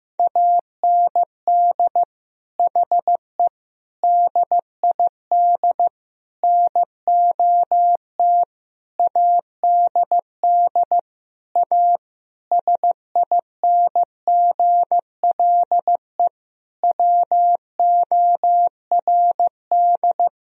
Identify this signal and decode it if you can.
Morse code, heard as a keyed tone, about 15 wpm, 700 Hz. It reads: AND HE DID NOT ADD A SINGLE WORD
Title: Sentences from Top 1000 Words - Review Part 1 - 15wpm